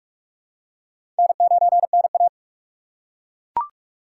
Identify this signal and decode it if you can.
Morse code, heard as a keyed tone, about 42 wpm, 700 Hz. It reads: N9NA
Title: USA Call Signs - 4 Characters - 3x Character Spacing - 45wpm